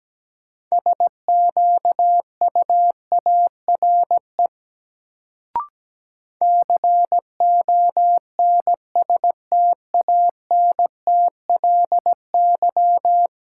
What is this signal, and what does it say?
Morse code, heard as a keyed tone, about 17 wpm, 700 Hz. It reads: SQUARE CONSTANTLY
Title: Sentences from Top 1000 Words - Review Part 7 - New Word Drill - 17wpm